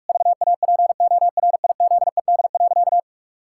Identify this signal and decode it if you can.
Morse code, heard as a keyed tone, about 41 wpm, 700 Hz. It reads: VAPORIZED.
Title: Pangram Challenge v2 - 45wpm